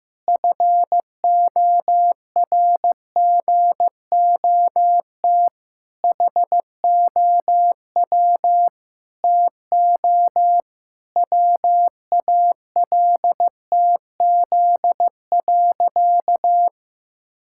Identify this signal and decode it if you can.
Morse code, heard as a keyed tone, about 15 wpm, 700 Hz. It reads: FORGOT HOW TO WALTZ.
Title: Pangram Challenge v2 - 15wpm